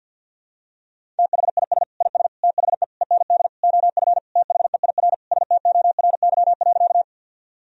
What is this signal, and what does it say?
Morse code, heard as a keyed tone, about 49 wpm, 700 Hz. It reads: THIS IS THE END OF THEIR STORY.
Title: Sentences from Top 1000 Words - Review Part 1 - 50wpm